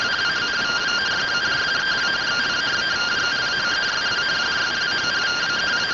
Two-tone teleprinter signal, about 170 Hz shift.